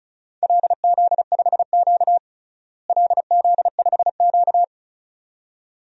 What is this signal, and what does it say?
Morse code, heard as a keyed tone, about 36 wpm, 700 Hz. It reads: LZ5Q LZ5Q
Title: DXCC-100 Call Signs - 4 Characters - Repeated 2x - 35wpm